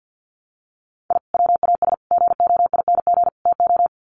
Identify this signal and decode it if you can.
Morse code, heard as a keyed tone, about 50 wpm, 700 Hz. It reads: I WAS GOING TO
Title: Sets of 4 Words 50wpm